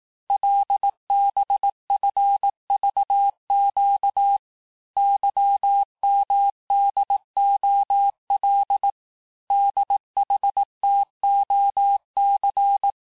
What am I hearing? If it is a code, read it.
Morse code, heard as a keyed tone, about 18 wpm, 800 Hz. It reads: LBFVQ YMDOL DHTOC
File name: cours-du-lundi files / cours-1588-texte-chiffre-vitesse-17MPM